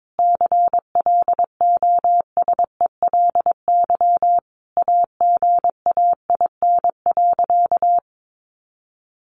Morse code, heard as a keyed tone, about 22 wpm, 700 Hz. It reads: CLOSELY AGAIN.
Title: Sentences from Top 1400 Words 22wpm